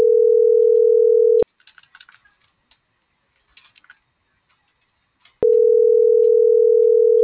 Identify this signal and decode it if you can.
Ringback tone (North America), 440 + 480 Hz.